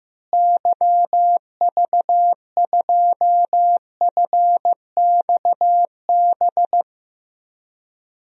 Morse code, heard as a keyed tone, about 15 wpm, 700 Hz. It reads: YV2FXB
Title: DXCC-100 Call Signs - 6 Characters - Repeated 3x - 15wpm